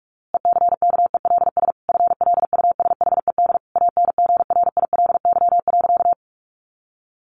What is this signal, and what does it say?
Morse code, heard as a keyed tone, about 44 wpm, 700 Hz. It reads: ECKELS FLUSHED ANGRILY.
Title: A Sound Of Thunder 45wpm